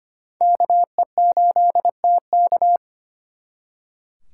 Morse code, heard as a keyed tone, about 34 wpm, spaced as for 30 wpm, 700 Hz. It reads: KE8TK
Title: Speed Racing - USA Call Signs 25wpm